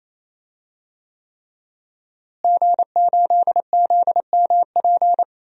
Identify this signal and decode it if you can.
Morse code, heard as a keyed tone, about 28 wpm, 700 Hz. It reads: G8ZMP
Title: DXCC-100 Call Signs - 5 Characters - Repeated 3x - 28wpm